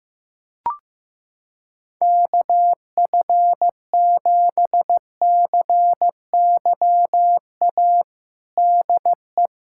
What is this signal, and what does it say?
Morse code, heard as a keyed tone, about 15 wpm, 700 Hz. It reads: KF7CYA DE
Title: USA Callsign DE Callsign - 6 Characters - 15wpm